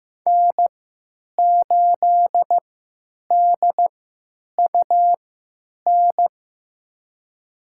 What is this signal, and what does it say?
Morse code, heard as a keyed tone, about 15 wpm, spaced as for 9 wpm, 700 Hz. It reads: N8DUN
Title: USA Call Signs - 5 Characters - 3x Character Spacing - 15wpm